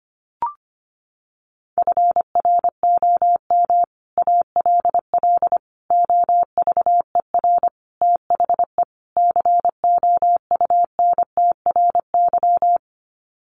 Morse code, heard as a keyed tone, about 25 wpm, 700 Hz. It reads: FROM ALL OVER THE COUNTRY
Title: Sets of 5 Words Encore 25wpm